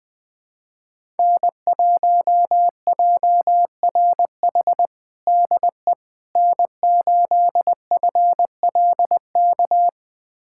Morse code, heard as a keyed tone, about 20 wpm, 700 Hz. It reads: N1JRH DE N8FLK
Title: USA Callsign DE Callsign - 5 Characters - 20wpm